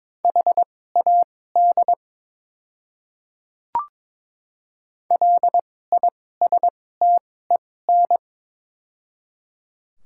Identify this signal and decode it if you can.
Morse code, heard as a keyed tone, about 22 wpm, spaced as for 16 wpm, 700 Hz. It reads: HAD LISTEN
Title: 2x Character Spacing - Top 500 Words - 22wpm